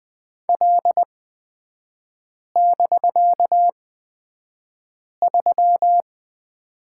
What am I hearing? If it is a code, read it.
Morse code, heard as a keyed tone, about 20 wpm, spaced as for 5 wpm, 700 Hz. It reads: L<BK>3
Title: 255 Lesson 41e - Period - Three Characters 20wpm